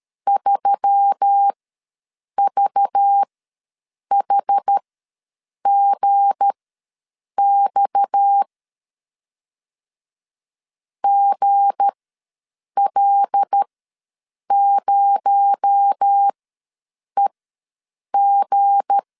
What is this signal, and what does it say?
Morse code, heard as a keyed tone, about 13 wpm, spaced as for 7 wpm, 800 Hz. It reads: 3VHGX GL0EG
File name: LEZIONI / 7 GF9Z / 7 G ws 12wpm 3140